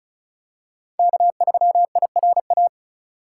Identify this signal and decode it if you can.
Morse code, heard as a keyed tone, about 36 wpm, 700 Hz. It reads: K3IRA
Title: USA Call Signs - 5 Characters - 3x Character Spacing - 35wpm